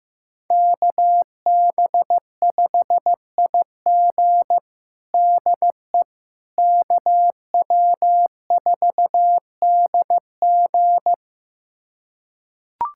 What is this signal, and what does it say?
Morse code, heard as a keyed tone, about 15 wpm, 700 Hz. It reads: KB5IG DE KW4DG
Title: USA Callsign DE Callsign - 5 Characters - 15wpm